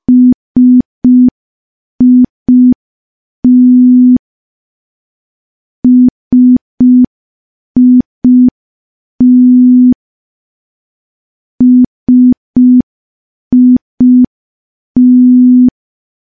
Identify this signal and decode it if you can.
Morse code, heard as a keyed tone, about 5 wpm, 250 Hz. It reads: SIT SIT SIT
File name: morse / morse-training-05WPM